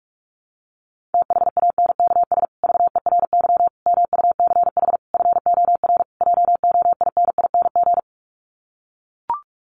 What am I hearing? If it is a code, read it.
Morse code, heard as a keyed tone, about 44 wpm, 700 Hz. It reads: THANKS VERY MUCH FOR JOINING
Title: Sets of 5 Words Encore2 45wpm